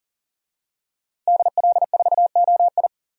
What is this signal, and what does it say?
Morse code, heard as a keyed tone, about 37 wpm, 700 Hz. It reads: DL4OI
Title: USA Call Signs - 5 Characters - Repeated 2x - 40wpm